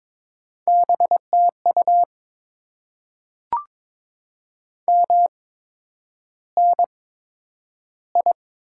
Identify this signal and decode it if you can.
Morse code, heard as a keyed tone, about 22 wpm, spaced as for 9 wpm, 690 Hz. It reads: BTU MNI
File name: Top 90 QSO Elements - 8x Character Spacing - 22wpm